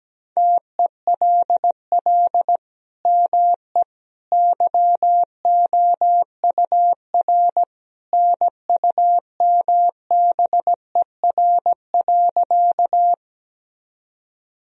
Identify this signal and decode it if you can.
Morse code, heard as a keyed tone, about 17 wpm, 700 Hz. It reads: TELL ME YOUR NUMBER.
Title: Sentences from Top 1000 Words - Review Part 1 - 17wpm